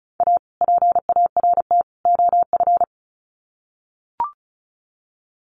Morse code, heard as a keyed tone, about 34 wpm, 700 Hz. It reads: A PART OF
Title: Sets of 3 Words 35wpm